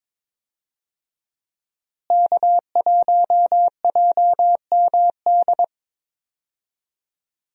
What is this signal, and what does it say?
Morse code, heard as a keyed tone, about 21 wpm, 700 Hz. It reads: K1JMD
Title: USA Call Signs - 5 Characters - Repeated 2x - 22wpm